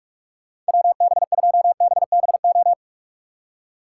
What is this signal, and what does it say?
Morse code, heard as a keyed tone, about 56 wpm, 700 Hz. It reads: WB2BBO